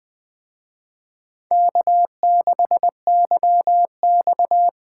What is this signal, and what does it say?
Morse code, heard as a keyed tone, about 21 wpm, 700 Hz. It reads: K6YX
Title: Speed Racing - USA Call Signs 17wpm